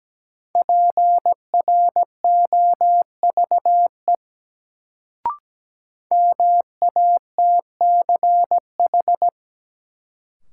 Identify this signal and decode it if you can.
Morse code, heard as a keyed tone, about 17 wpm, 700 Hz. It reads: PROVE MATCH
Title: Sentences from Top 1600 Words - New Word Drill - 17wpm